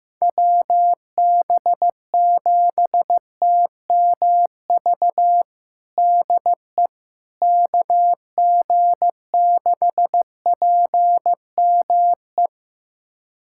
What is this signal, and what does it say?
Morse code, heard as a keyed tone, about 15 wpm, 700 Hz. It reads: WB7TMV DE KG6PME
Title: USA Callsign DE Callsign 15wpm